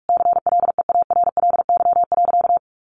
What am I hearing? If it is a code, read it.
Morse code, heard as a keyed tone, about 44 wpm, 700 Hz. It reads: CLEARLY.
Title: Sentences from Top 1300 Words 45wpm